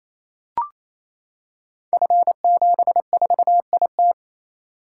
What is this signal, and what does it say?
Morse code, heard as a keyed tone, about 29 wpm, 700 Hz. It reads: F74IT
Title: Code Groups - 5 Letter:Numbers - 28wpm